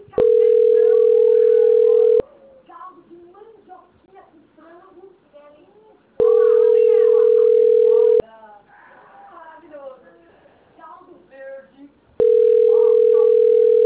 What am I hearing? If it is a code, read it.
Ringback tone (North America), 440 + 480 Hz, with speech.
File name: rg-1600-unknown- - - .77733